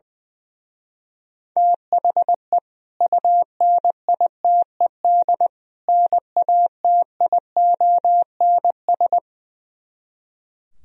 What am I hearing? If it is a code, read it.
Morse code, heard as a keyed tone, about 21 wpm, 700 Hz. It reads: THE UNITED NATIONS
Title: Speed-Racing - Sets of 3 Words 20wpm